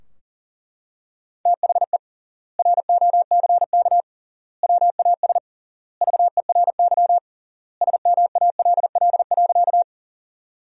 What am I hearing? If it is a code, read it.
Morse code, heard as a keyed tone, about 40 wpm, 700 Hz. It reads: THE ROCK WAS VERY SMALL.
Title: 3x Word-Spacing - Sentences from Top 400 Words - 40wpm